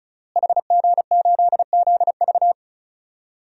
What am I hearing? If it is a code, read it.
Morse code, heard as a keyed tone, about 34 wpm, 690 Hz. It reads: HG8ZV
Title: DXCC-100 Call Signs - 5 Characters - Repeated 3x - 35wpm